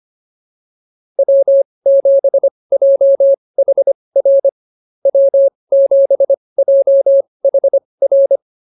Morse code, heard as a keyed tone, about 25 wpm, spaced as for 20 wpm, 590 Hz. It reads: W7JHR W7JHR